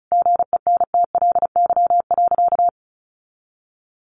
Morse code, heard as a keyed tone, about 34 wpm, 700 Hz. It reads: GENTLY.
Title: Sentences from Top 1500 Words 35wpm